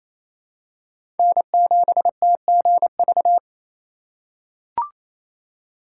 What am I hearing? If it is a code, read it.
Morse code, heard as a keyed tone, about 28 wpm, 700 Hz. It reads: N7TGV
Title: USA Call Signs - 8x Character Spacing - 28wpm